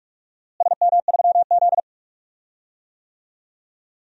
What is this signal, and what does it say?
Morse code, heard as a keyed tone, about 46 wpm, 700 Hz. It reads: SM3Z
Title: DXCC-100 Call Signs - 4 Characters - Repeated 2x - 45wpm